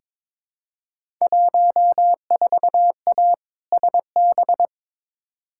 Morse code, heard as a keyed tone, about 22 wpm, 700 Hz. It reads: 14A SB
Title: Field Day Exchanges Encore - 22wpm